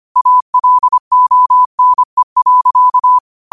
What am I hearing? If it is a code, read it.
Morse code, heard as a keyed tone, about 25 wpm, 1000 Hz. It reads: ALONE.
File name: qotdfiles / 25wpm